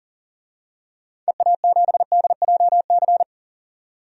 Morse code, heard as a keyed tone, about 40 wpm, 700 Hz. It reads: EA7DJC
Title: DXCC-100 Call Signs - 6 Characters - 40wpm